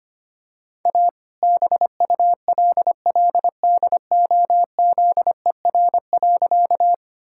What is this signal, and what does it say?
Morse code, heard as a keyed tone, about 25 wpm, 700 Hz. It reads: A BULLDOZER.
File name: practice / 25wpm / All-Age Appropriate Jokes - 25wpm